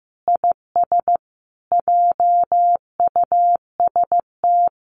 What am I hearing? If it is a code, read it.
Morse code, heard as a keyed tone, about 15 wpm, 700 Hz. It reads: IS JUST
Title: Sentences from Top 1500 Words 15wpm